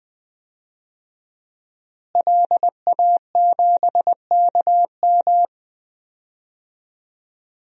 Morse code, heard as a keyed tone, about 20 wpm, 700 Hz. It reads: LA7KM
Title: DXCC-100 Call Signs - 5 Characters - Repeated 2x - 20wpm